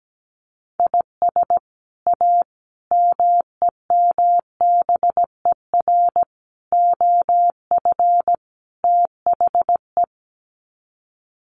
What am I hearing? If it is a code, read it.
Morse code, heard as a keyed tone, about 17 wpm, 700 Hz. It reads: IS A MEMBER OF THE
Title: Sets of 5 Words 17wpm